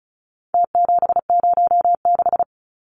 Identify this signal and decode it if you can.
Morse code, heard as a keyed tone, about 34 wpm, 700 Hz. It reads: T706
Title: Numbers - Contest Style - 4-Digit 35wpm